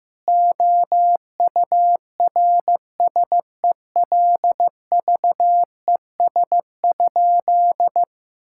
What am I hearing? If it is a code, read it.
Morse code, heard as a keyed tone, about 15 wpm, 700 Hz. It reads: OURSELVES?
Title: Sentences from Top 1600 Words 15wpm